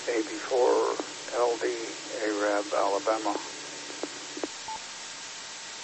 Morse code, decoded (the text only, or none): none